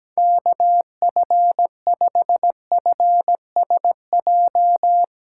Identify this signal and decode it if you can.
Morse code, heard as a keyed tone, about 17 wpm, 700 Hz. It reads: KF5FSJ
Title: Speed Racing - USA 2x3 Call Signs 17wpm